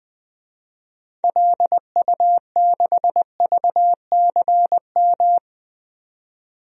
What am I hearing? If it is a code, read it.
Morse code, heard as a keyed tone, about 21 wpm, 700 Hz. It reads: LU6VCM
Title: DXCC-100 Call Signs - 6 Characters - 20wpm